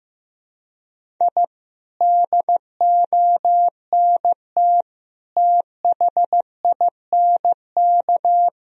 Morse code, heard as a keyed tone, about 16 wpm, 700 Hz. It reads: I DONT THINK
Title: Sentences from Top 1000 Words - Review Part 4 - 15wpm